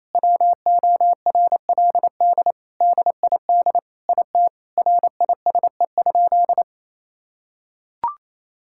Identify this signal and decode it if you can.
Morse code, heard as a keyed tone, about 27 wpm, 700 Hz. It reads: WORLD DID IT RISE?
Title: Sentences from Top 1000 Words - Review Part 6 - 28wpm